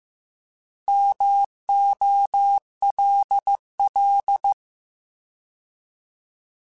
Morse code, heard as a keyed tone, about 15 wpm, 800 Hz. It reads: MOLL
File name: Alphabetical / Alphabetical mp3s / Ms @13